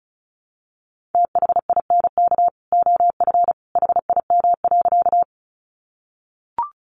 Morse code, heard as a keyed tone, about 34 wpm, 700 Hz. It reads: THINK OF HIM.
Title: Sentences from Top 200 Words 35wpm